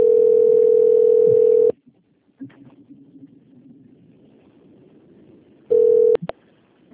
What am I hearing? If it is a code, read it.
Ringback tone (North America), 440 + 480 Hz.